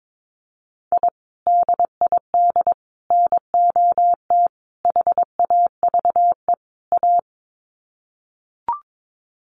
Morse code, heard as a keyed tone, about 22 wpm, 700 Hz. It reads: I DID NOT HAVE A
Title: Sets of 5 Words 22wpm